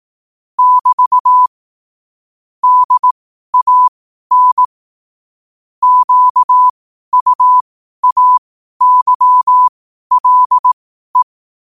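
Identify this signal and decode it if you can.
Morse code, heard as a keyed tone, about 18 wpm, spaced as for 12 wpm, 1000 Hz. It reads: <BT> DAN QUAYLE
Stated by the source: Morse code sent at 12 words per minute.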